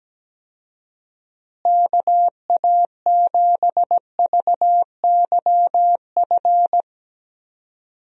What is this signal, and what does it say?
Morse code, heard as a keyed tone, about 17 wpm, 700 Hz. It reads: KA7VYF
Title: USA Call Signs - 6 Characters - Repeated 3x - 17wpm